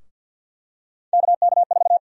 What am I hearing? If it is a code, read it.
Morse code, heard as a keyed tone, about 47 wpm, 700 Hz. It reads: KK4
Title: USA Call Sign Prefixes - 50wpm